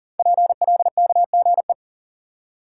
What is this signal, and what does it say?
Morse code, heard as a keyed tone, about 40 wpm, 690 Hz. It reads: PLKGE
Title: Code Groups - 5 Letters - 40wpm